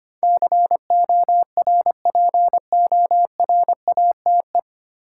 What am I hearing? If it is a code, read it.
Morse code, heard as a keyed tone, about 25 wpm, 700 Hz. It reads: CORPORATE
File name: practice / 25wpm / 9 Letter Words - First 3 Letters Given - 25wpm